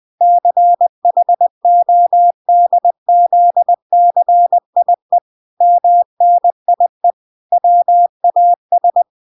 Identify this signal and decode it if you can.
Morse code, heard as a keyed tone, about 20 wpm, 700 Hz. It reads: CHODZCIE MNIE WAS
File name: recordings / 20wpm0000